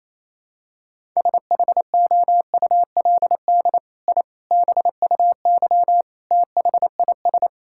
Morse code, heard as a keyed tone, about 28 wpm, 700 Hz. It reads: SHOULD I BUY THIS
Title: Sentences from Top 1000 Words - Review Part 5 - 28wpm